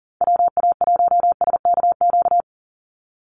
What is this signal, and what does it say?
Morse code, heard as a keyed tone, about 40 wpm, 700 Hz. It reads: WA1SKQ
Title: Call Signs Repeated 2x - 40wpm